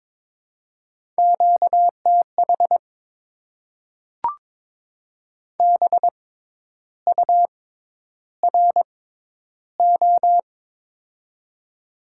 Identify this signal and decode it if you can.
Morse code, heard as a keyed tone, about 22 wpm, spaced as for 10 wpm, 700 Hz. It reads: QTH BURO
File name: Top 90 QSO Elements - 6x Character Spacing - 22wpm